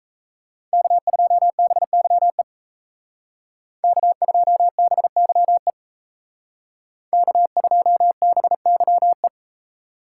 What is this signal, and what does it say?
Morse code, heard as a keyed tone, about 37 wpm, 700 Hz. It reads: K2BYE K2BYE K2BYE